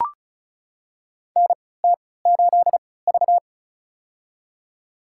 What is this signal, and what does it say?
Morse code, heard as a keyed tone, about 36 wpm, spaced as for 20 wpm, 700 Hz. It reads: NT8V
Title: USA Call Signs - 3x Character Spacing - 35wpm